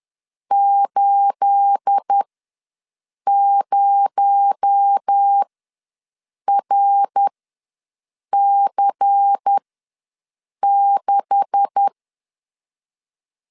Morse code, heard as a keyed tone, about 10 wpm, spaced as for 6 wpm, 800 Hz. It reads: 80RC6
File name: LEZIONI / 9 286 ws 10wpm 4032